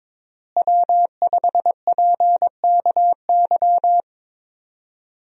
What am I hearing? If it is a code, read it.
Morse code, heard as a keyed tone, about 22 wpm, 700 Hz. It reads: W5PKY
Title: USA Call Signs - 5 Characters - Repeated 3x - 22wpm